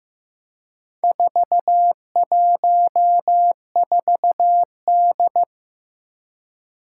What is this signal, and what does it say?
Morse code, heard as a keyed tone, about 15 wpm, 700 Hz. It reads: 414D
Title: Four Challenging Letter-Number Combinations 15wpm